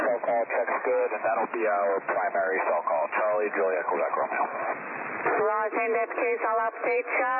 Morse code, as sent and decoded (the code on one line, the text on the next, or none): none